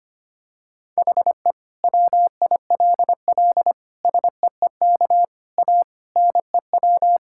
Morse code, heard as a keyed tone, about 25 wpm, 700 Hz. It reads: HE WILL SEEK A NEW
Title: Sentences from Top 1600 Words 25wpm